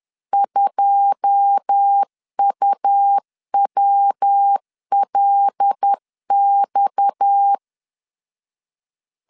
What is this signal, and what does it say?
Morse code, heard as a keyed tone, about 10 wpm, 800 Hz. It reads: 2UWLX
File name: CW / DET / 1 9 1500C S B 10wpm 4439